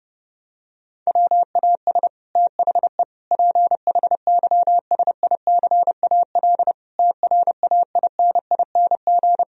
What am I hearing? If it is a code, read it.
Morse code, heard as a keyed tone, about 30 wpm, 700 Hz. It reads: WAS THE PHYSICAL TRAINING
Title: Sentences from Top 1000 Words - Review Part 4 - 30wpm